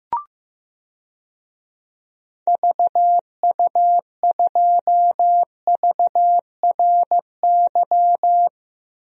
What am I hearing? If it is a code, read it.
Morse code, heard as a keyed tone, about 15 wpm, 700 Hz. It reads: VU2VRY
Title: DXCC-100 Call Signs - 6 Characters - Repeated 2x - 15wpm